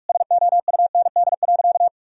Morse code, heard as a keyed tone, about 41 wpm, 700 Hz. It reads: SOUND.
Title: Sentences from Top 1000 Words - Review Part 1 - 45wpm